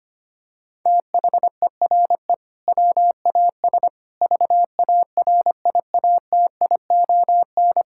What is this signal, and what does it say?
Morse code, heard as a keyed tone, about 25 wpm, 700 Hz. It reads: THERE WAS VARIATION
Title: Sentences from Top 1600 Words 25wpm